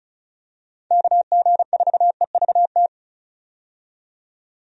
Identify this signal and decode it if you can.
Morse code, heard as a keyed tone, about 34 wpm, 700 Hz. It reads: KG4EVT
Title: USA Call Signs - 6 Characters - 35wpm